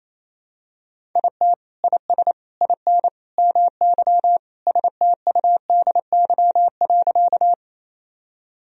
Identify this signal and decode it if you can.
Morse code, heard as a keyed tone, about 28 wpm, 700 Hz. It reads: IT IS IN MY STUDY.
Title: Sentences from Top 1000 Words - Review Part 1 - 28wpm